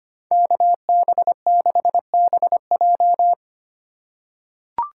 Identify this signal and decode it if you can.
Morse code, heard as a keyed tone, about 25 wpm, 700 Hz. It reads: KB6BJ
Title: USA Call Signs - 5 Characters - 2x Character Spacing - 25wpm